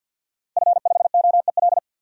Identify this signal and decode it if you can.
Morse code, heard as a keyed tone, about 52 wpm, 700 Hz. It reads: F5OEL
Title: DXCC-100 Call Signs - 5 Characters - Repeated 2x - 50wpm